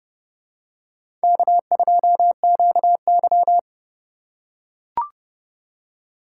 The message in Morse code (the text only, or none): K2QY